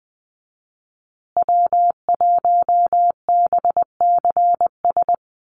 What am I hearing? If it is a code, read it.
Morse code, heard as a keyed tone, about 20 wpm, 700 Hz. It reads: W1BCS
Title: USA Call Signs Repeated 3x 20wpm